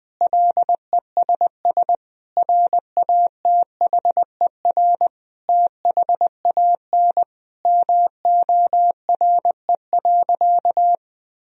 Morse code, heard as a keyed tone, about 20 wpm, 700 Hz. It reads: LESS RATHER THAN MORE.
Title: Sentences from Top 1000 Words - Review Part 4 - 20wpm